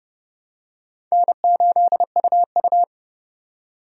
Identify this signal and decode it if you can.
Morse code, heard as a keyed tone, about 31 wpm, 700 Hz. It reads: N8UU
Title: USA Call Signs - 4 Characters - 30wpm